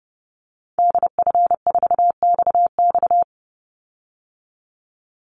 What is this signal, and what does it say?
Morse code, heard as a keyed tone, about 30 wpm, 700 Hz. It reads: DF4XX
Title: Call Signs Repeated 3x - 30wpm